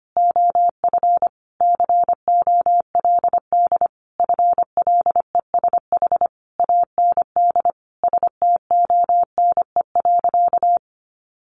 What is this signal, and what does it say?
Morse code, heard as a keyed tone, about 25 wpm, 700 Hz. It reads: OF COLD FLESH AND STONE.
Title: A Sound Of Thunder 25wpm